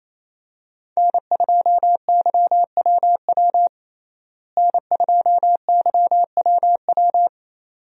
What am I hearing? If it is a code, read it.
Morse code, heard as a keyed tone, about 28 wpm, 700 Hz. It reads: N2YWW N2YWW
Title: USA Call Signs - 5 Characters - Repeated 2x - 28wpm